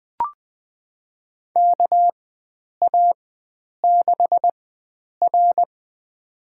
Morse code, heard as a keyed tone, about 20 wpm, spaced as for 9 wpm, 700 Hz. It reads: KA6R
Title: USA Call Signs - 4 Characters - 4x Character Spacing - 20wpm